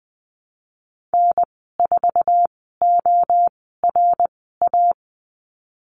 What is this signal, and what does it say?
Morse code, heard as a keyed tone, about 20 wpm, spaced as for 14 wpm, 700 Hz. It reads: N4ORA
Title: 055 Lesson 10g - Letter r - Call Signs 20wpm